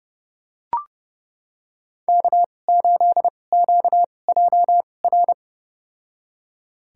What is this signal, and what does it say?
Morse code, heard as a keyed tone, about 30 wpm, spaced as for 22 wpm, 700 Hz. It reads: K8QJR
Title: USA Call Signs - 5 Characters - 2x Character Spacing - 30wpm